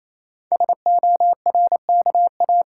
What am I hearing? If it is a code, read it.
Morse code, heard as a keyed tone, about 27 wpm, 700 Hz. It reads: SORKA
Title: Code Groups - 5 Letters - 28wpm